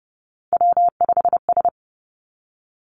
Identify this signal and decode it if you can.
Morse code, heard as a keyed tone, about 30 wpm, 700 Hz. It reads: W5S
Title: Three Challenging Letter-Number Combinations 30wpm